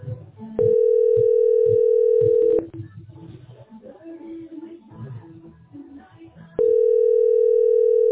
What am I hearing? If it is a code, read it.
Ringback tone (North America), 440 + 480 Hz.